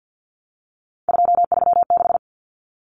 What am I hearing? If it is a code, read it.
Morse code, heard as a keyed tone, about 50 wpm, 700 Hz. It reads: 236
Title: Numbers - 3-Digit 50wpm